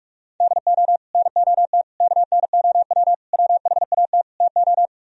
Morse code, heard as a keyed tone, about 47 wpm, 700 Hz. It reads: DO NOT KNOW WHAT TO